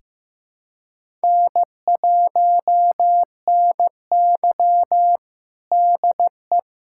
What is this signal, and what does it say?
Morse code, heard as a keyed tone, about 15 wpm, 700 Hz. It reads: N1NY DE
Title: USA Callsign DE Callsign - 4 Characters - 15wpm